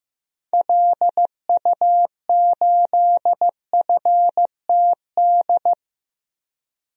Morse code, heard as a keyed tone, about 15 wpm, 700 Hz. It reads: LU8FTD
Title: DXCC-100 Call Signs - 6 Characters - Repeated 2x - 15wpm